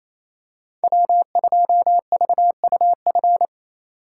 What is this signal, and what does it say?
Morse code, heard as a keyed tone, about 27 wpm, 700 Hz. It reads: W2VUF
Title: K1USN SST QSO Practice - 28wpm